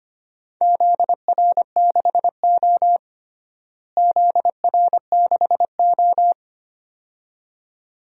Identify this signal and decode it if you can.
Morse code, heard as a keyed tone, about 25 wpm, 700 Hz. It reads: ZR6O ZR6O
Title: DXCC-100 Call Signs - 4 Characters - Repeated 2x - 25wpm